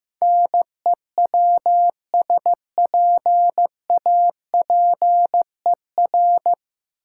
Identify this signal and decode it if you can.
Morse code, heard as a keyed tone, about 16 wpm, 700 Hz. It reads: NEWSPAPER
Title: Sentences from Top 1100 Words - New Word Drill Alt - 15wpm